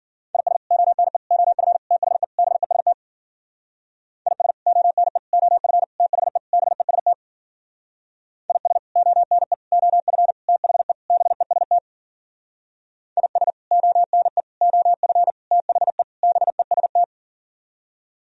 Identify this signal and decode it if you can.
Morse code, heard as a keyed tone, about 51 wpm, 700 Hz. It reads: IS ONE OF THE BEST IS ONE OF THE BEST IS ONE OF THE BEST IS ONE OF THE BEST